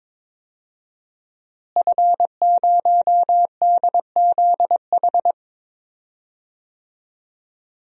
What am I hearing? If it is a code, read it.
Morse code, heard as a keyed tone, about 22 wpm, 700 Hz. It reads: F0DZH
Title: DXCC-100 Call Signs - 5 Characters - Repeated 2x - 22wpm